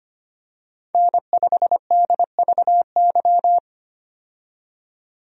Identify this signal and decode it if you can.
Morse code, heard as a keyed tone, about 25 wpm, 690 Hz. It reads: N5DVY
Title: USA Call Signs - 5 Characters - 25wpm